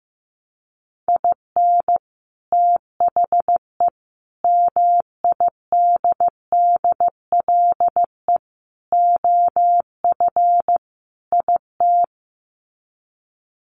Morse code, heard as a keyed tone, about 15 wpm, 700 Hz. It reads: IN THE MIDDLE OF IT
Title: Sets of 5 Words Encore 15wpm